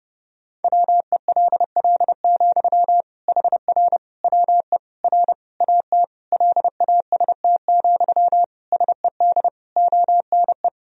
Morse code, heard as a keyed tone, about 31 wpm, 700 Hz. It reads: WELL, HR WE R AT LAST, SED ONE
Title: Sentences from Top 300 Words and Abbreviations 30wpm